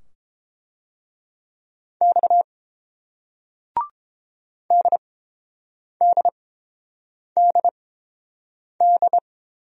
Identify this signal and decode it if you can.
Morse code, heard as a keyed tone, about 30 wpm, spaced as for 7 wpm, 700 Hz. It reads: X DDDD